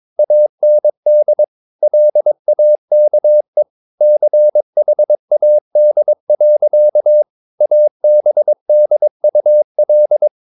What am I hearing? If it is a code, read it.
Morse code, heard as a keyed tone, about 22 wpm, 600 Hz. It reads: AND LAKE CHAD. ABDUL
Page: Romanen Never af Ken Follet på engelsk.